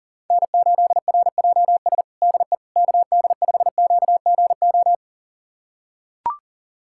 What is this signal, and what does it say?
Morse code, heard as a keyed tone, about 40 wpm, 700 Hz. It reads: N8RJS DE KD5QGO
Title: USA Callsign DE Callsign 40wpm